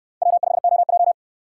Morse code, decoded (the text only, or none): W5OJ